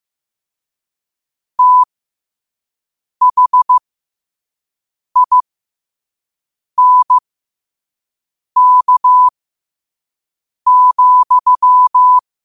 Morse code, heard as a keyed tone, about 15 wpm, spaced as for 5 wpm, 1000 Hz. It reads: THINK,